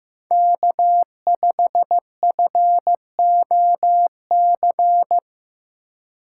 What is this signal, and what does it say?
Morse code, heard as a keyed tone, about 15 wpm, 700 Hz. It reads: K5FOC
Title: USA Call Signs - 5 Characters - 3x Character Spacing - 15wpm